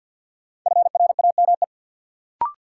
Morse code, heard as a keyed tone, about 51 wpm, 700 Hz. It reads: FRAME
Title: Sentences from Top 1400 Words - New Word Drill Alt - 50wpm